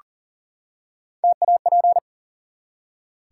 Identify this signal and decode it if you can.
Morse code, heard as a keyed tone, about 42 wpm, 700 Hz. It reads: TAP
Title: Three Letter Words - 40wpm